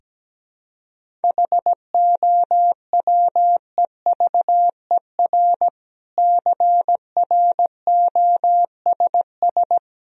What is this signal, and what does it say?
Morse code, heard as a keyed tone, about 17 wpm, 700 Hz. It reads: HOWEVER CROSS
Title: Sentences from Top 1000 Words - Review Part 1 - 17wpm